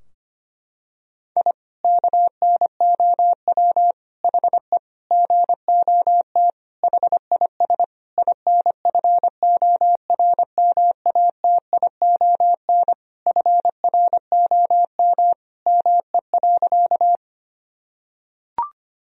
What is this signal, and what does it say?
Morse code, heard as a keyed tone, about 25 wpm, 700 Hz. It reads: I KNOW HE GOT HIS INFORMATION FROM ME.
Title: Sentences from Top 1000 Words - Review Part 3 - 25wpm